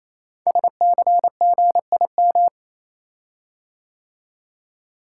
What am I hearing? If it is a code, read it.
Morse code, heard as a keyed tone, about 29 wpm, 700 Hz. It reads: SCGIM
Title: Code Groups - 5 Letter:Numbers - 28wpm